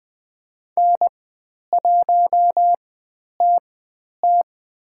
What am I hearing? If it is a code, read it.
Morse code, heard as a keyed tone, about 21 wpm, spaced as for 10 wpm, 700 Hz. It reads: N1TT
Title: 109 Lesson 18g - Letter m - Call Signs 20wpm